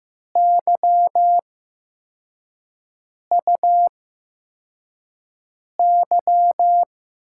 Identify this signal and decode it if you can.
Morse code, heard as a keyed tone, about 15 wpm, spaced as for 4 wpm, 700 Hz. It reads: YUY
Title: USA Call Signs - 5 Characters - 8x Character Spacing - 15wpm